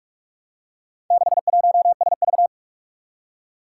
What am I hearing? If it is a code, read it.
Morse code, heard as a keyed tone, about 46 wpm, 700 Hz. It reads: 61SV
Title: Four Challenging Letter-Number Combinations 45wpm